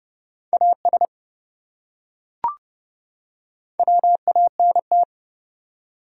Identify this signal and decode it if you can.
Morse code, heard as a keyed tone, about 27 wpm, 700 Hz. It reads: AS WANT